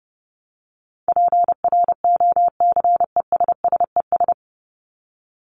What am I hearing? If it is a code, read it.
Morse code, heard as a keyed tone, about 30 wpm, 700 Hz. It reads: PROCESSES
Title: ES:S Suffix - Top 100 Words with Base Words 30wpm